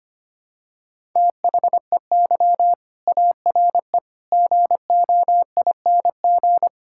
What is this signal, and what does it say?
Morse code, heard as a keyed tone, about 25 wpm, 700 Hz. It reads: THEY ARE GOING
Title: Sentences from Top 1600 Words 25wpm